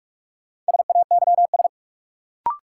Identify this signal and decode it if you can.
Morse code, heard as a keyed tone, about 41 wpm, 700 Hz. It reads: SAYS
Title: Sets of 2 Words - New Word Drill - 45wpm